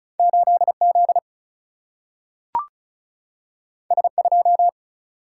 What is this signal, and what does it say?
Morse code, heard as a keyed tone, about 36 wpm, 700 Hz. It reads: 8Z S2
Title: Single-Letter Single-Number 35wpm